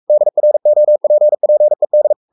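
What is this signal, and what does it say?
Morse code, heard as a keyed tone, about 44 wpm, 600 Hz. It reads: DROPPED
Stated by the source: Morse code at 43 wpm sending Ken Follett's novel Never in English, from CW med Gnister.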